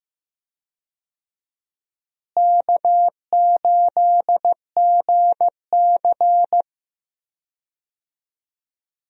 Morse code, heard as a keyed tone, about 15 wpm, 700 Hz. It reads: K8GC
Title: USA Call Signs - 4 Characters - Repeated 3x - 15wpm